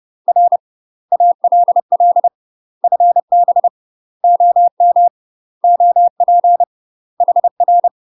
Morse code, heard as a keyed tone, about 30 wpm, 700 Hz. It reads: R ALL FB OM OP HR